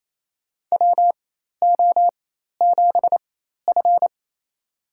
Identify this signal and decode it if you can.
Morse code, heard as a keyed tone, about 28 wpm, spaced as for 13 wpm, 700 Hz. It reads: WO7F
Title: USA Call Signs - 4 Characters - 4x Character Spacing - 28wpm